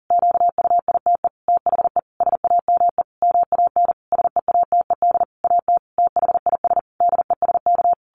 Morse code, heard as a keyed tone, about 40 wpm, 700 Hz. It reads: QUITE THE SAME MAN SEATED AT THIS DESK
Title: A Sound Of Thunder 40wpm